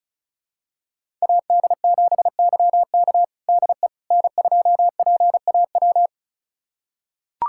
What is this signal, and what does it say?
Morse code, heard as a keyed tone, about 36 wpm, 700 Hz. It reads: AD7YK DE N2PAW
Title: USA Callsign DE Callsign - 5 Characters - 35wpm